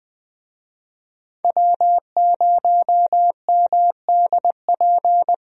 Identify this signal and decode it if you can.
Morse code, heard as a keyed tone, about 20 wpm, 700 Hz. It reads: W0MDP
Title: USA Call Signs - 5 Characters - 20wpm